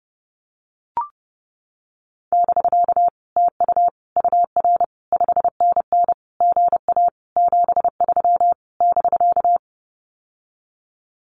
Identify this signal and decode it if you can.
Morse code, heard as a keyed tone, about 30 wpm, 700 Hz. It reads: <BK> TU UR 5NN GA 73 <BK>
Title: Common QSO Phrases Encore 30wpm